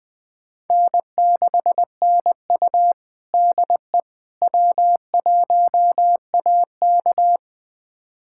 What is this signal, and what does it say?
Morse code, heard as a keyed tone, about 20 wpm, 700 Hz. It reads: N6NU DE W1AK
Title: USA Callsign DE Callsign - 4 Characters - 20wpm